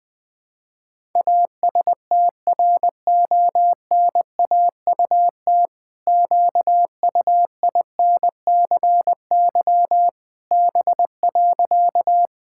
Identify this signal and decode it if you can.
Morse code, heard as a keyed tone, about 20 wpm, 700 Hz. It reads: ASTRONAUT QUINCY B.
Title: Pangram Challenge v2 - 20wpm